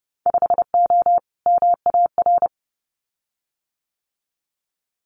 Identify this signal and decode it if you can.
Morse code, heard as a keyed tone, about 32 wpm, 700 Hz. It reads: 5O MAR
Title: Winter Field Day Exchanges Encore 30wpm